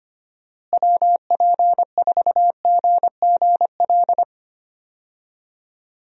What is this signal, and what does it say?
Morse code, heard as a keyed tone, about 25 wpm, 700 Hz. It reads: WP4GGL
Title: USA Call Signs - 6 Characters - Repeated 3x - 25wpm